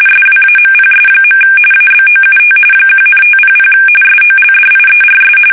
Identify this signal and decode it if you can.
RTTY, about 45.45 baud. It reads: SWAMPS. ABOUT 1,500 SOLDIERS
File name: fsk850 / usnavy2